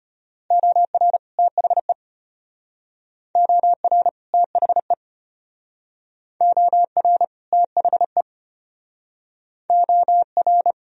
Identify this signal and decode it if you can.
Morse code, heard as a keyed tone, about 32 wpm, 700 Hz. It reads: OR THE OR THE OR THE OR